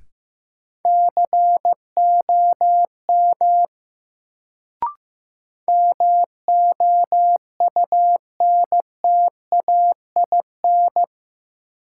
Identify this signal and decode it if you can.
Morse code, heard as a keyed tone, about 15 wpm, 700 Hz. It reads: COM MOUNTAIN
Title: Sentences from Top 1300 Words - New Word Drill Alt - 15wpm